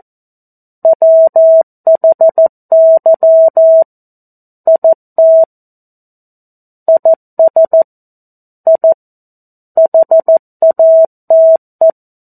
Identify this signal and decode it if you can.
Morse code, heard as a keyed tone, about 14 wpm, 650 Hz. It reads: WHY IT IS I HATE
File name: 02-Chapter-02 14 WPM